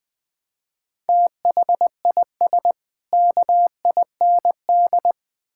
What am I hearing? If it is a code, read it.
Morse code, heard as a keyed tone, about 20 wpm, 700 Hz. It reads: THIS KIND
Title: Build A Phrase - 3-Word Phrases - 20wpm